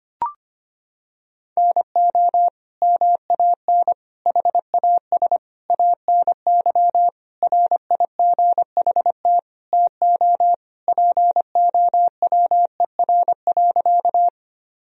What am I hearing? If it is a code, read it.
Morse code, heard as a keyed tone, about 25 wpm, 700 Hz. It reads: NO MAN HAS ANY RIGHT TO POWER.
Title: Sentences from Top 1000 Words - Review Part 2 - 25wpm